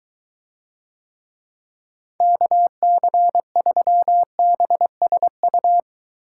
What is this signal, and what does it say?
Morse code, heard as a keyed tone, about 22 wpm, 700 Hz. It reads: KC3BSU